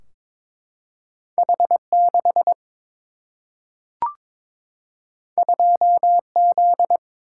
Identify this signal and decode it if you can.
Morse code, heard as a keyed tone, about 21 wpm, 700 Hz. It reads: H6 2Z
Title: Single-Letter Single-Number 22wpm